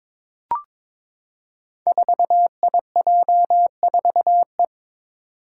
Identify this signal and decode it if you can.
Morse code, heard as a keyed tone, about 21 wpm, 700 Hz. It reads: 4IJ4E
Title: Five Challenging Letter-Number Combinations 22wpm